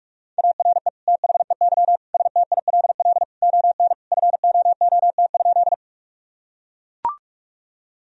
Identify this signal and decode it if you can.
Morse code, heard as a keyed tone, about 41 wpm, 700 Hz. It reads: ARE THEY STILL ON FOOT?
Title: Sentences from Top 1000 Words - Review Part 3 - 45wpm